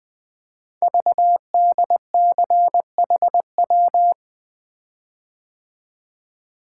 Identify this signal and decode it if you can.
Morse code, heard as a keyed tone, about 20 wpm, 700 Hz. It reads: VDCHW
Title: Code Groups - 5 Letters - 20wpm